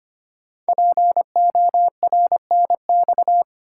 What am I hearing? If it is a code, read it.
Morse code, heard as a keyed tone, about 25 wpm, 700 Hz. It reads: PORNX
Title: Code Groups - 5 Letters - 25wpm